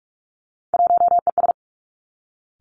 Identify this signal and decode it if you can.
Morse code, heard as a keyed tone, about 46 wpm, 700 Hz. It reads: 1ES
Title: Three Challenging Letter-Number Combinations 45wpm